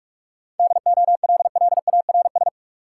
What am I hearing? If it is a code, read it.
Morse code, heard as a keyed tone, about 42 wpm, 700 Hz. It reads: DOLLARS
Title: Sets of 5 Words - New Word Drill - 45wpm